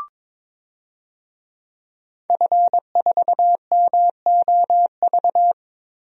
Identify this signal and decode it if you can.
Morse code, heard as a keyed tone, about 22 wpm, 700 Hz. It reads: F4MOV
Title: DXCC-100 Call Signs - 5 Characters - 22wpm